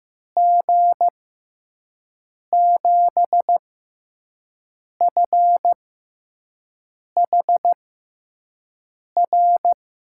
Morse code, heard as a keyed tone, about 15 wpm, spaced as for 5 wpm, 700 Hz. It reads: G7FHR
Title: USA Call Signs - 6 Characters - 6x Character Spacing - 15wpm